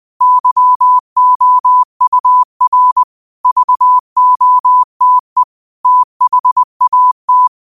Morse code, heard as a keyed tone, about 20 wpm, 1000 Hz. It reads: YOUR VOTE THAT
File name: 2025-06-25-20wpm